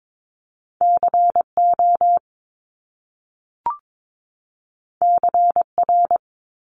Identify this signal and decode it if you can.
Morse code, heard as a keyed tone, about 22 wpm, 700 Hz. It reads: CO CR
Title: Top Two Letter Combinations in Words 22wpm